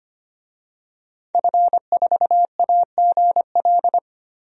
Morse code, heard as a keyed tone, about 25 wpm, 700 Hz. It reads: F4AGL
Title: DXCC-100 Call Signs - 5 Characters - 25wpm